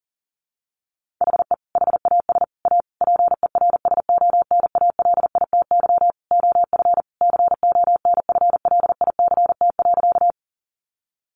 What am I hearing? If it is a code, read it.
Morse code, heard as a keyed tone, about 40 wpm, 700 Hz. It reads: HE HAS A PERSONALITY OF CONFLICT.
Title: Sentences from Top 1400 Words 40wpm